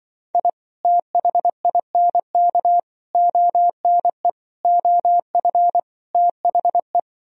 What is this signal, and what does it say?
Morse code, heard as a keyed tone, about 24 wpm, 700 Hz. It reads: I THINK ONE OF THE
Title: Speed-Racing - Sets of 5 Words 20wpm